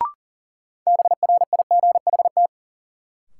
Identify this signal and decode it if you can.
Morse code, heard as a keyed tone, about 40 wpm, 700 Hz. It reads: BRIGHT
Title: Sentences from Top 1000 Words - New Word Drill - 40wpm